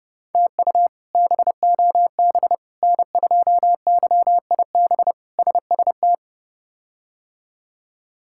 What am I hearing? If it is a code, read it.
Morse code, heard as a keyed tone, about 32 wpm, 700 Hz. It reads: TU BOB N2YIB SST
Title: K1USN SST QSO Practice - 30wpm